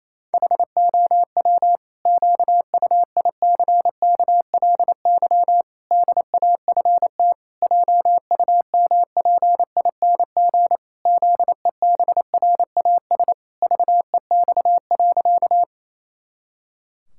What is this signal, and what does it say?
Morse code, heard as a keyed tone, about 28 wpm, 700 Hz. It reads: HOW QUICKLY DAFT JUMPING ZEBRAS VEX.
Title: Pangram Challenge v2 - 28wpm